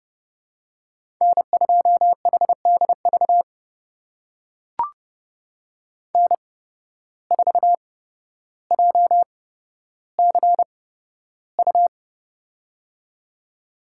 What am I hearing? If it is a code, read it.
Morse code, heard as a keyed tone, about 30 wpm, spaced as for 13 wpm, 700 Hz. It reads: N2HDV N4JCU
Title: USA Call Signs - 5 Characters - 8x Character Spacing - 30wpm